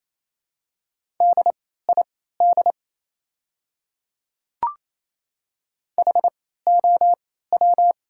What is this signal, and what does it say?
Morse code, heard as a keyed tone, about 27 wpm, spaced as for 16 wpm, 700 Hz. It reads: DID HOW
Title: 3x Character Spacing - Top 100 Words - 28wpm